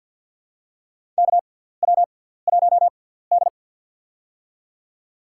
Morse code, heard as a keyed tone, about 47 wpm, spaced as for 17 wpm, 700 Hz. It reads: KW1D
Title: USA Call Signs - 6x Character Spacing - 50wpm